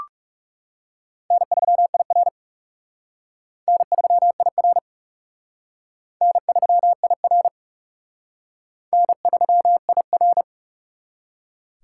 Morse code, heard as a keyed tone, about 36 wpm, 700 Hz. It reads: N3IR N3IR N3IR N3IR
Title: Speed Racing - USA 1x2 Call Signs 30wpm